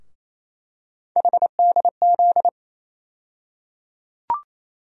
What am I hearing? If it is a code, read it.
Morse code, heard as a keyed tone, about 28 wpm, 700 Hz. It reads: HDZ
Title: USA Call Sign Postfixes - 3 Letters 28wpm